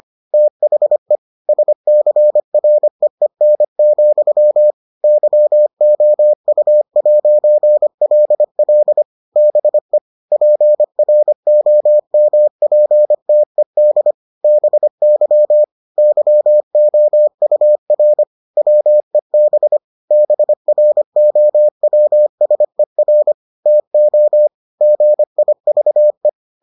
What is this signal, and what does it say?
Morse code, heard as a keyed tone, about 25 wpm, 600 Hz. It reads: THE SCREEN, YOU'LL BE PROMPTED BY YOUR WEB BROWSER TO GIVE